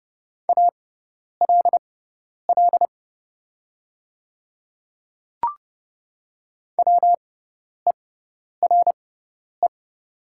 Morse code, heard as a keyed tone, about 30 wpm, spaced as for 10 wpm, 700 Hz. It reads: ALL WERE
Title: 6x Character Spacing - Top 100 Words - 30wpm